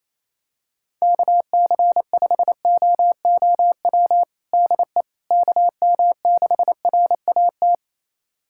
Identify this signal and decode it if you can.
Morse code, heard as a keyed tone, about 28 wpm, 700 Hz. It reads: KC5OOW DE KM6RAT
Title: USA Callsign DE Callsign - 6 Characters - 28wpm